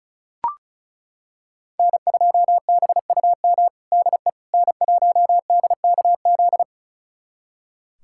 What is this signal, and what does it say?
Morse code, heard as a keyed tone, about 34 wpm, 700 Hz. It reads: N2BUM DE N1DKZ
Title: USA Callsign DE Callsign - 5 Characters - 35wpm